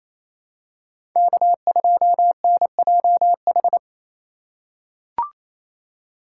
Morse code, heard as a keyed tone, about 28 wpm, 700 Hz. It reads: K2NJH
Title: USA Call Signs - 2x Character Spacing - 28wpm